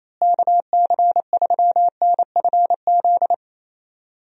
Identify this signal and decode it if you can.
Morse code, heard as a keyed tone, about 28 wpm, 700 Hz. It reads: KC3NFZ
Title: USA Call Signs - 6 Characters - Repeated 3x - 28wpm